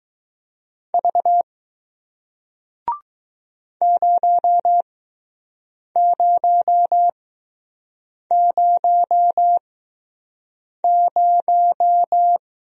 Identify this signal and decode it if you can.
Morse code, heard as a keyed tone, about 20 wpm, spaced as for 6 wpm, 700 Hz. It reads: V 0000